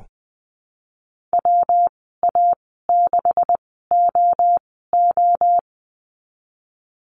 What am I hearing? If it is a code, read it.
Morse code, heard as a keyed tone, about 20 wpm, spaced as for 15 wpm, 700 Hz. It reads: WA6OO
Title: 154 Lesson 25g - Letter p - Call Signs 20wpm